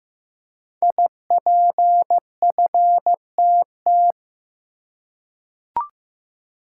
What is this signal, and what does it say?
Morse code, heard as a keyed tone, about 15 wpm, 700 Hz. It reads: IPFTT
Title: Code Groups - 5 Letters - 15wpm